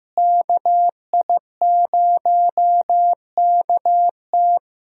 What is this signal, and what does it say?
Morse code, heard as a keyed tone, about 15 wpm, 700 Hz. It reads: KI0KT
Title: USA Call Signs - 5 Characters - Repeated 3x - 15wpm